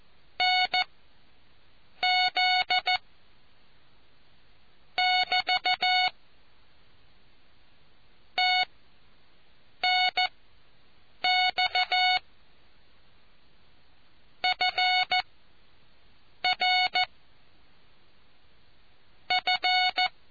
Morse code, heard as a keyed tone, about 14 wpm, spaced as for 6 wpm, 740 Hz. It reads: NZ <BT> TNX FR F